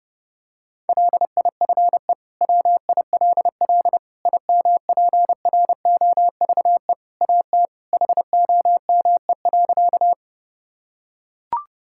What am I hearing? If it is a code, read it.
Morse code, heard as a keyed tone, about 30 wpm, 700 Hz. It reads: LIFE WILL IMPROVE AT HOME.
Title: Sentences from Top 1000 Words - Review Part 5 - 30wpm